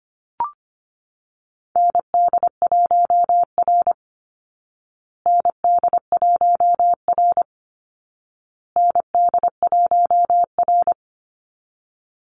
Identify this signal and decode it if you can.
Morse code, heard as a keyed tone, about 25 wpm, 700 Hz. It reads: ND1R ND1R ND1R
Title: Call Signs Repeated 2x - 25wpm